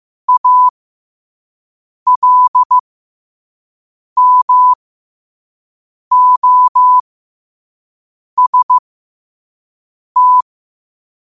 Morse code, heard as a keyed tone, about 15 wpm, spaced as for 5 wpm, 1000 Hz. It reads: ALMOST
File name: qotdfiles / 5wpm